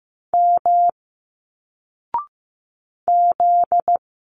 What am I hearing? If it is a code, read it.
Morse code, heard as a keyed tone, about 15 wpm, 700 Hz. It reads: M Z
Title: Single Letters 15wpm